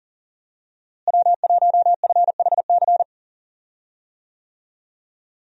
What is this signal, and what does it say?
Morse code, heard as a keyed tone, about 37 wpm, 700 Hz. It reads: W1FHC
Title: USA Call Signs - 5 Characters - Repeated 3x - 40wpm